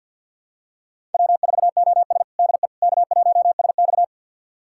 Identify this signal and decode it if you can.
Morse code, heard as a keyed tone, about 49 wpm, 700 Hz. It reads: W4OS DE K1SX
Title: USA Callsign DE Callsign - 4 Characters - 50wpm